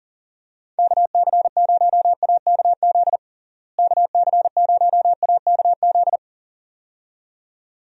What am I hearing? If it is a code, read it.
Morse code, heard as a keyed tone, about 40 wpm, 700 Hz. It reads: KC0AKZ KC0AKZ
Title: USA Call Signs - 6 Characters - Repeated 2x - 40wpm